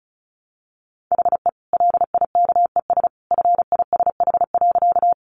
Morse code, heard as a keyed tone, about 34 wpm, 700 Hz. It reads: HE LIKES FISH.
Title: Additional Easy Phrases 35wpm